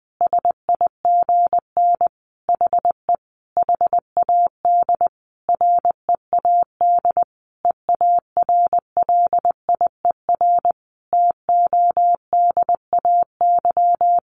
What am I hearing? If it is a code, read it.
Morse code, heard as a keyed tone, about 19 wpm, 700 Hz. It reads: SIGN HE HAD READ EARLIER TODAY
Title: A Sound Of Thunder 20wpm